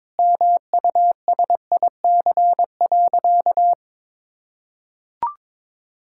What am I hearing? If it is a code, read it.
Morse code, heard as a keyed tone, about 22 wpm, 700 Hz. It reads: MUSIC.
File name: practice / 22wpm / Sentences with 3-Word Phrases - 22wpm